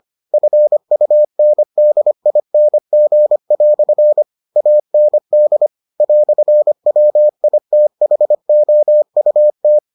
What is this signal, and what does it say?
Morse code, heard as a keyed tone, about 25 wpm, 600 Hz. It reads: FUNDING" AND "WITHOUT